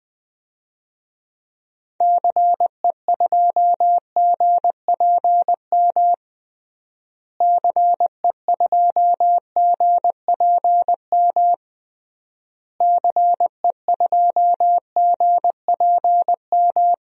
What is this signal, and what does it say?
Morse code, heard as a keyed tone, about 20 wpm, 700 Hz. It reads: CE2GPM CE2GPM CE2GPM
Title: DXCC-100 Call Signs - 6 Characters - Repeated 3x - 20wpm